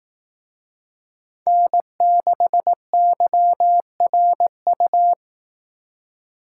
Morse code, heard as a keyed tone, about 18 wpm, 700 Hz. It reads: N6YRU
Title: Speed Racing - USA 1x3 Call Signs 15wpm